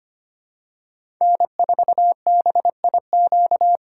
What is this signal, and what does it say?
Morse code, heard as a keyed tone, about 25 wpm, 700 Hz. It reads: N4BIQ
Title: USA Call Signs - 5 Characters - 3x Character Spacing - 25wpm